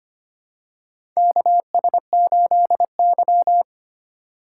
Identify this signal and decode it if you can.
Morse code, heard as a keyed tone, about 25 wpm, 700 Hz. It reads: KS8Y
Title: USA Call Signs - 4 Characters - 8x Character Spacing - 25wpm